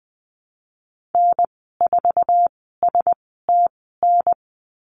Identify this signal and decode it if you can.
Morse code, heard as a keyed tone, about 19 wpm, spaced as for 14 wpm, 700 Hz. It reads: N4STN
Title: 055 Lesson 10g - Letter r - Call Signs 20wpm